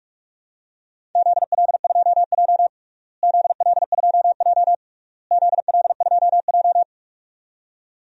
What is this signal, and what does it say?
Morse code, heard as a keyed tone, about 41 wpm, 700 Hz. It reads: ZL2J ZL2J ZL2J
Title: DXCC-100 Call Signs - 4 Characters - Repeated 3x - 45wpm